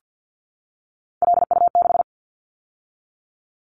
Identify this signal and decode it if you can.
Morse code, heard as a keyed tone, about 50 wpm, 700 Hz. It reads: LU6
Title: Callsign Prefixes for Top 50 DXCC - 50wpm